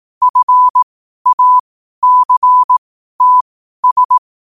Morse code, heard as a keyed tone, about 18 wpm, spaced as for 12 wpm, 1000 Hz. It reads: FACTS
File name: qotdfiles / 12wpm